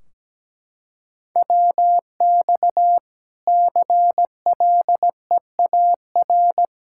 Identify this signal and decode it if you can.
Morse code, heard as a keyed tone, about 17 wpm, 700 Hz. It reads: WX CLEAR
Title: Practice QSO - Exchange 2 - Info - 17wpm